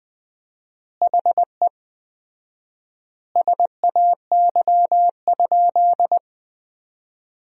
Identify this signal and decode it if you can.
Morse code, heard as a keyed tone, about 21 wpm, 700 Hz. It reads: HE SAY?
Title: 4x Word-Spacing - Sentences from Top 200 Words - 20wpm